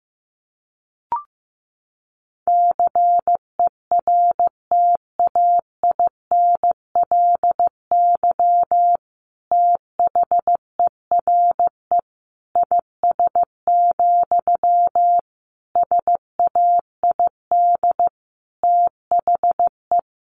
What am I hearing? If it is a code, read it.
Morse code, heard as a keyed tone, about 15 wpm, 700 Hz. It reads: CERTAINLY THERE IS, SAID THE
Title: Sentences from Top 1100 Words 15wpm